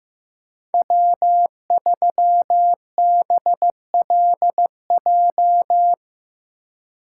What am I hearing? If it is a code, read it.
Morse code, heard as a keyed tone, about 15 wpm, 700 Hz. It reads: W3BLJ
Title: USA Call Signs - 5 Characters - 4x Character Spacing - 15wpm